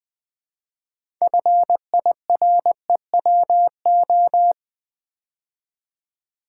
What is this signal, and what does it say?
Morse code, heard as a keyed tone, about 20 wpm, 700 Hz. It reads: FIREWO
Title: Build A Word - 2 Letters At A Time - 8-Letter Words - 20wpm